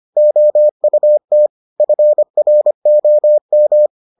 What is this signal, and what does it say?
Morse code, heard as a keyed tone, about 25 wpm, 600 Hz. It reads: OUT FROM